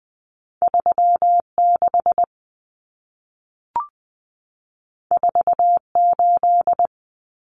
Morse code, heard as a keyed tone, about 20 wpm, 700 Hz. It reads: 36 48
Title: Common Cut Numbers - 2-Digit 20wpm